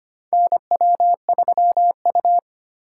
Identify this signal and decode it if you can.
Morse code, heard as a keyed tone, about 25 wpm, 690 Hz. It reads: NW3U
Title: USA Call Signs - 4 Characters - 4x Character Spacing - 25wpm